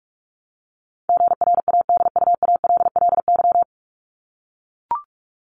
Morse code, heard as a keyed tone, about 44 wpm, 700 Hz. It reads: GRADUALLY
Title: LY Suffix - Top 100 Words with Base Words 45wpm